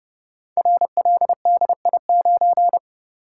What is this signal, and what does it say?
Morse code, heard as a keyed tone, about 30 wpm, 700 Hz. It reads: RLDI9
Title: Code Groups - 5 Letter:Numbers - 30wpm